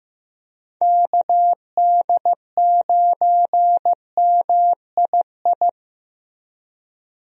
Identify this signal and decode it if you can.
Morse code, heard as a keyed tone, about 15 wpm, 700 Hz. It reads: KD9MII
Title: USA Call Signs - 6 Characters - Repeated 2x - 15wpm